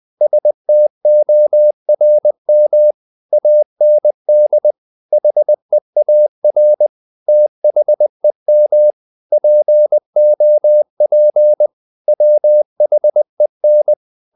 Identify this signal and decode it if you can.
Morse code, heard as a keyed tone, about 20 wpm, 600 Hz. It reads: STORM AND HEAR THEM POP WHEN
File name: the old man and the sea / the old man and the sea part 1 20wpm